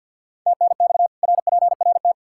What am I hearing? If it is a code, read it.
Morse code, heard as a keyed tone, about 52 wpm, 700 Hz. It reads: TNX RPRT
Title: K1USN SST QSO Practice 50wpm